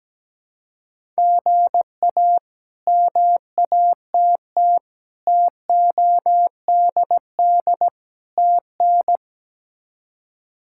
Morse code, heard as a keyed tone, about 17 wpm, 700 Hz. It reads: GA MATT TODD TN
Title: K1USN SST QSO Practice - 17wpm